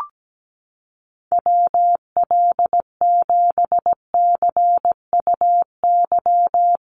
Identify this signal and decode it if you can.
Morse code, heard as a keyed tone, about 17 wpm, 700 Hz. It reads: WL7CUY
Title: USA Call Signs Repeated 2x 17wpm